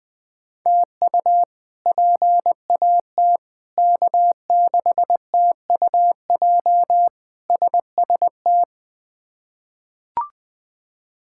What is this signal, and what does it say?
Morse code, heard as a keyed tone, about 21 wpm, 700 Hz. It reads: TU PAT K6TUJ SST
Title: K1USN SST QSO Practice - 20wpm